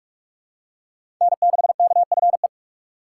Morse code, heard as a keyed tone, about 43 wpm, 700 Hz. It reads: N6KFE
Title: USA Call Signs - 5 Characters - 2x Character Spacing - 45wpm